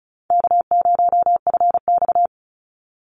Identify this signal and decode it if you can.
Morse code, heard as a keyed tone, about 34 wpm, 700 Hz. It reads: K0FX
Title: Easy Call Signs Repeated 2x - 35wpm